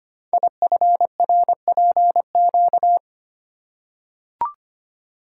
Morse code, heard as a keyed tone, about 25 wpm, 700 Hz. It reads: IFRPQ
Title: Code Groups - 5 Letter:Numbers - 25wpm